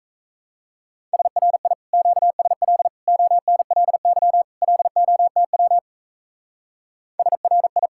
Morse code, heard as a keyed tone, about 34 wpm, 700 Hz. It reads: SRI QSL ONLY LOTW SRI
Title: Speed-Racing Common QSO Phrases Encore - 28wpm